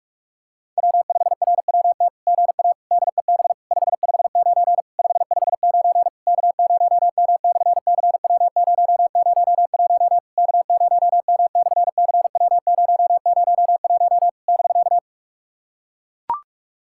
Morse code, heard as a keyed tone, about 42 wpm, 700 Hz. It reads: W5RWT GA DEB 559 559 K0M/CW001 K0M/CW001 <BK>
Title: SOTA QSO Practice 45wpm